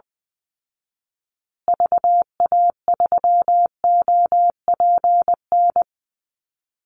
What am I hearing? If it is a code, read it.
Morse code, heard as a keyed tone, about 20 wpm, 700 Hz. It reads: VA3OPN
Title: Call Signs Repeated 3x - 20wpm